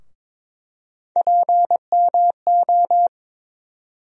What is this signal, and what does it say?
Morse code, heard as a keyed tone, about 22 wpm, 700 Hz. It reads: PMO
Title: USA Call Sign Postfixes - 3 Letters 22wpm